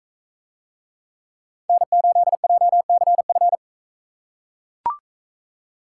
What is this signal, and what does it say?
Morse code, heard as a keyed tone, about 39 wpm, 700 Hz. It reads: N8JCF